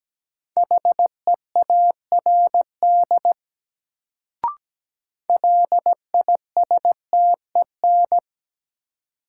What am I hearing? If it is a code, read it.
Morse code, heard as a keyed tone, about 17 wpm, 700 Hz. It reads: HEARD LISTEN
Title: Sentences from Top 400 Words - New Word Drill - 17wpm